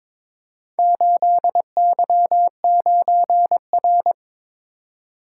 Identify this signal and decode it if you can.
Morse code, heard as a keyed tone, about 22 wpm, 700 Hz. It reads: 8Y9R
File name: practice / 22wpm / Four Letter-Number Combinations - 22wpm